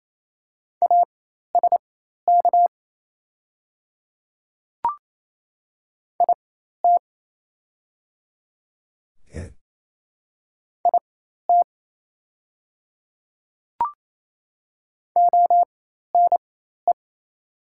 Morse code, heard as a keyed tone, about 27 wpm, spaced as for 13 wpm, 700 Hz. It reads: ASK IT IT ONE
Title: 4x Character Spacing - Top 200 Words - 28wpm